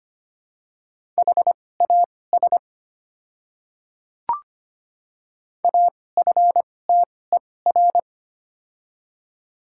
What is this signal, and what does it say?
Morse code, heard as a keyed tone, about 25 wpm, spaced as for 18 wpm, 700 Hz. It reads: HAS AFTER
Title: 2x Character Spacing - Top 200 Words - 25wpm